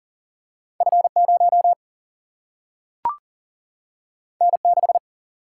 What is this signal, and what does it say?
Morse code, heard as a keyed tone, about 39 wpm, 700 Hz. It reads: F 0 N 6
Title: Single-Letter Single-Number 40wpm